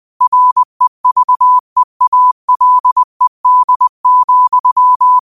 Morse code, heard as a keyed tone, about 20 wpm, 1000 Hz. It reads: REVEALED,
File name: qotdfiles / 20wpm